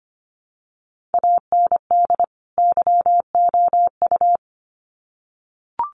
Morse code, heard as a keyed tone, about 25 wpm, 700 Hz. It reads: AND YOU
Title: Sets of 2 Words 25wpm